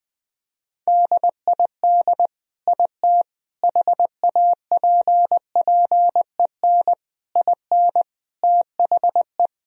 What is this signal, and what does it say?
Morse code, heard as a keyed tone, about 21 wpm, 700 Hz. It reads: DID IT HAPPEN IN THE
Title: Sentences from Top 1000 Words - Review Part 2 - 20wpm